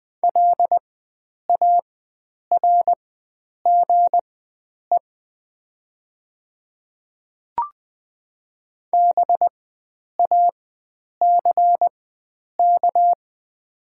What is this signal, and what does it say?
Morse code, heard as a keyed tone, about 20 wpm, spaced as for 9 wpm, 700 Hz. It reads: LARGE BACK
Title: 4x Character Spacing - Top 300 Words - 20wpm